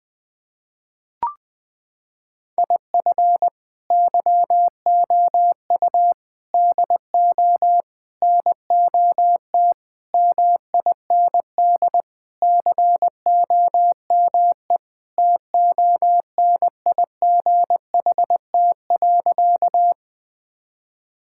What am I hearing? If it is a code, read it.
Morse code, heard as a keyed tone, about 21 wpm, 700 Hz. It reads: IF YOU DO NOT MIND COME TONIGHT.
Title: Sentences with 4-Word Phrases - 20wpm